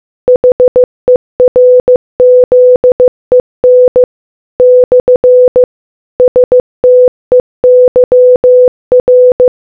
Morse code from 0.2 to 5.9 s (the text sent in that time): HERZEN /